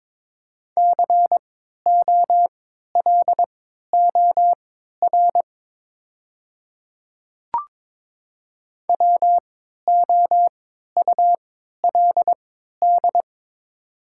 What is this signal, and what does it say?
Morse code, heard as a keyed tone, about 22 wpm, spaced as for 12 wpm, 700 Hz. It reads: COLOR WOULD
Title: 3x Character Spacing - Top 500 Words - 22wpm